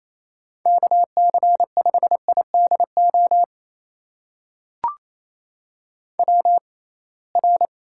Morse code, heard as a keyed tone, about 28 wpm, spaced as for 21 wpm, 700 Hz. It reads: KC5IDO WR
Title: USA Call Signs - 6x Character Spacing - 28wpm